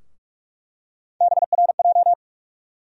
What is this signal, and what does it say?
Morse code, heard as a keyed tone, about 46 wpm, 700 Hz. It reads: BRJ
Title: USA Call Sign Postfixes - 3 Letters 45wpm